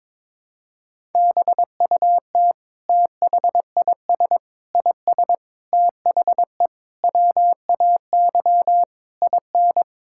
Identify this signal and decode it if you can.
Morse code, heard as a keyed tone, about 22 wpm, 700 Hz. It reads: BUT THIS IS THE WAY IN
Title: Sentences from Top 1000 Words - Review Part 5 - 22wpm